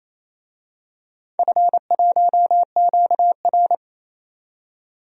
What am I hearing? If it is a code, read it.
Morse code, heard as a keyed tone, about 27 wpm, 700 Hz. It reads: F1QR
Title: DXCC-100 Call Signs - 4 Characters - 28wpm